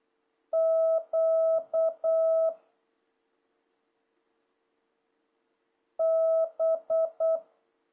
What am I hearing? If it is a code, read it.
Morse code, heard as a keyed tone, about 8 wpm, 650 Hz. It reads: Q B
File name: jh8cbh / 2026cw